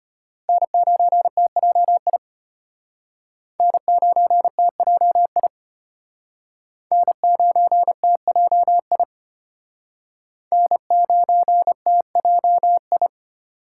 Morse code, heard as a keyed tone, about 34 wpm, 700 Hz. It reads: N9TJI N9TJI N9TJI N9TJI